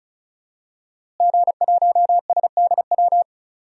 Morse code, heard as a keyed tone, about 34 wpm, 700 Hz. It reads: G1SDW
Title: DXCC-100 Call Signs - 5 Characters - Repeated 3x - 35wpm